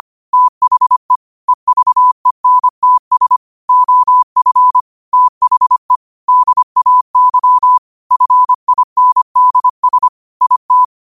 Morse code, heard as a keyed tone, about 25 wpm, 1000 Hz. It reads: THE EVENTS OF THE DAY FINDS IT
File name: qotdfiles / 25wpm